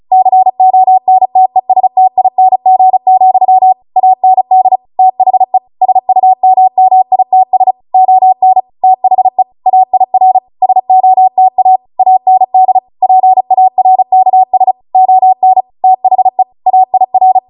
Morse code, heard as a keyed tone, about 36 wpm, 750 Hz. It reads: CONTESTING, AND THE SUMMITS ON THE AIR SOTA AND PARKS ON THE AIR